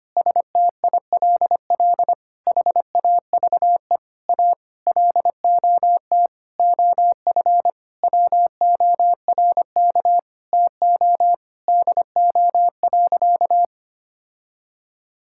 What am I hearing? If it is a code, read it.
Morse code, heard as a keyed tone, about 25 wpm, 700 Hz. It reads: STILL HAVE A LOT OF WORK TO DO.
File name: practice / 25wpm / Sentences with 9-Word Phrases - 25wpm